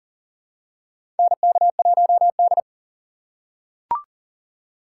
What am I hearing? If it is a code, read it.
Morse code, heard as a keyed tone, about 40 wpm, 700 Hz. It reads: NK1D
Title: USA Call Signs - 4 Characters - 3x Character Spacing - 40wpm